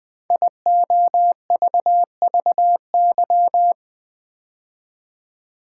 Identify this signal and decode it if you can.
Morse code, heard as a keyed tone, about 20 wpm, 700 Hz. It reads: IOVVY
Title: Code Groups - 5 Letter:Numbers - 20wpm